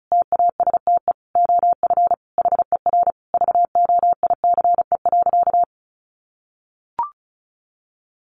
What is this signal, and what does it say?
Morse code, heard as a keyed tone, about 34 wpm, 700 Hz. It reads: TASTE OF HER VOICE.
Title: Sentences from Top 1100 Words 35wpm